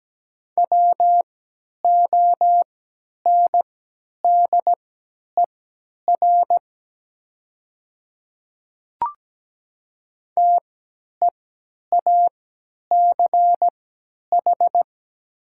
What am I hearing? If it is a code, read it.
Morse code, heard as a keyed tone, about 17 wpm, spaced as for 10 wpm, 700 Hz. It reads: WONDER TEACH
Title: 3x Character Spacing - Top 500 Words - 17wpm